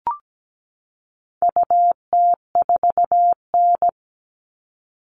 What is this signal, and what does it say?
Morse code, heard as a keyed tone, about 17 wpm, 700 Hz. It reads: UT4N
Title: All Cut Numbers - 4-Digit 17wpm